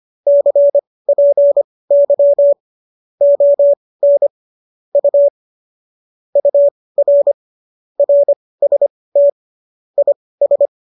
Morse code, heard as a keyed tone, about 25 wpm, spaced as for 18 wpm, 570 Hz. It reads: CPY ON U UR RST IS
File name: qso215 18